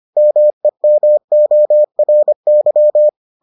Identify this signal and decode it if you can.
Morse code, heard as a keyed tone, about 25 wpm, 600 Hz. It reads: MEMORY